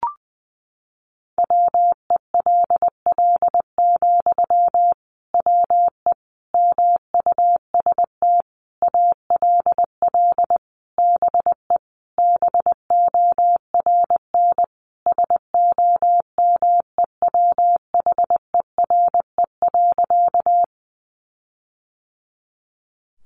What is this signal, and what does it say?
Morse code, heard as a keyed tone, about 19 wpm, 700 Hz. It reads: WELL, WE MUST ALL BE BORN SOMEWHERE.
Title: Sentences from Top 1400 Words 20wpm